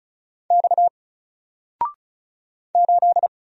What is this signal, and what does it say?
Morse code, heard as a keyed tone, about 36 wpm, 700 Hz. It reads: X 8
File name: Single-Letter-Number-35wpm